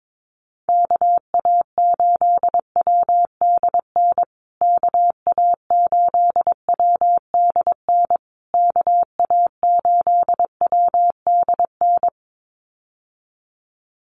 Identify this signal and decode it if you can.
Morse code, heard as a keyed tone, about 22 wpm, 700 Hz. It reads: KA8WDN KA8WDN KA8WDN
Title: USA Call Signs Repeated 3x 22wpm